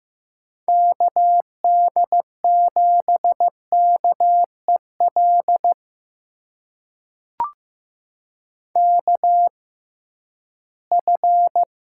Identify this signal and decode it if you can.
Morse code, heard as a keyed tone, about 15 wpm, spaced as for 11 wpm, 700 Hz. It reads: KD7KEL KF
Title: USA Call Signs - 6 Characters - 6x Character Spacing - 15wpm